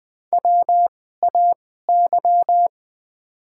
Morse code, heard as a keyed tone, about 20 wpm, spaced as for 14 wpm, 700 Hz. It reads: WAY
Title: 2x Character Spacing - Top 100 Words - 20wpm